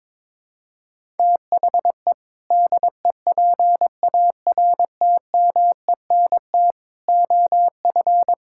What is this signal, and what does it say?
Morse code, heard as a keyed tone, about 22 wpm, 700 Hz. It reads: THE DEPARTMENT OF
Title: Sentences from Top 1000 Words - Review Part 5 - 22wpm